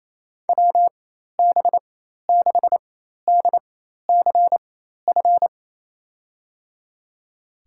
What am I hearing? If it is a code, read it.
Morse code, heard as a keyed tone, about 27 wpm, spaced as for 13 wpm, 700 Hz. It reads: WB6DCF
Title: USA Call Signs - 6 Characters - 4x Character Spacing - 28wpm